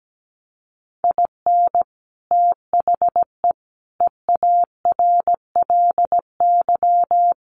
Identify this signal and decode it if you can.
Morse code, heard as a keyed tone, about 17 wpm, 700 Hz. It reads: IN THE EARLY
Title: Sets of 5 Words Encore 17wpm